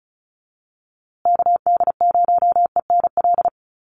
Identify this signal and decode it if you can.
Morse code, heard as a keyed tone, about 34 wpm, 700 Hz. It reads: KD0ENL
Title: USA Call Signs Repeated 3x 35wpm